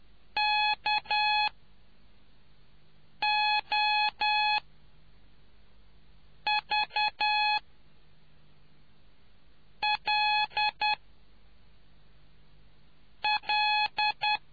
Morse code, heard as a keyed tone, about 10 wpm, spaced as for 4 wpm, 840 Hz. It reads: KOVL L